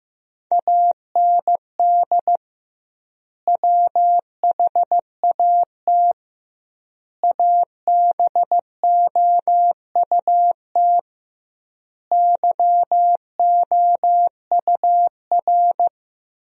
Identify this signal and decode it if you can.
Morse code, heard as a keyed tone, about 15 wpm, 700 Hz. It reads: AND WHAT ABOUT YOUR
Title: 2x Word-Spacing - Sentences from Top 200 Words - 15wpm